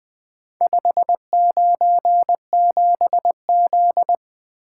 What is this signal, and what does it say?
Morse code, heard as a keyed tone, about 20 wpm, 700 Hz. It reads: 597Z
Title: Four Letter-Number Combinations - 20wpm